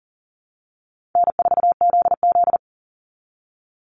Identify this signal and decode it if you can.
Morse code, heard as a keyed tone, about 40 wpm, 700 Hz. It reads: N4ZZ
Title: Call Signs Repeated 3x - 40wpm